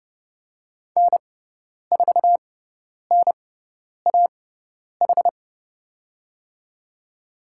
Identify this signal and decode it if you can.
Morse code, heard as a keyed tone, about 31 wpm, spaced as for 10 wpm, 700 Hz. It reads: N4NAH
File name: Lesson 17g - Letter u - Call Signs 30wpm